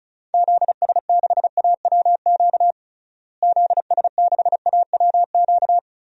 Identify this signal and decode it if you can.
Morse code, heard as a keyed tone, about 36 wpm, 700 Hz. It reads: ZS6AWQ ZS6AWQ
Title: DXCC-100 Call Signs - 6 Characters - Repeated 3x - 35wpm